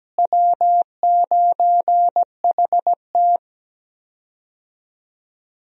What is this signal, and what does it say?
Morse code, heard as a keyed tone, about 17 wpm, 700 Hz. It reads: W9HT
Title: USA Call Signs - 4 Characters - Repeated 3x - 17wpm